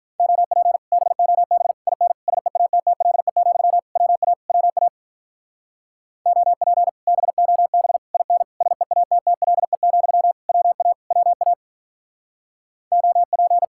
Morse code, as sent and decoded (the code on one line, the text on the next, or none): --- .--. / -... --- -... / .. -. / ... .- - - .-.. --..-- / .-- .- / .-- .- / --- .--. / -... --- -... / .. -. / ... .- - - .-.. --..-- / .-- .- / .-- .- / --- .--.
OP BOB IN SEATTLE, WA WA OP BOB IN SEATTLE, WA WA OP